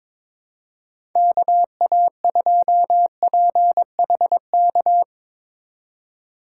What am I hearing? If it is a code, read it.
Morse code, heard as a keyed tone, about 24 wpm, 700 Hz. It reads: KA2PHK
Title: Speed Racing - USA 2x3 Call Signs 22wpm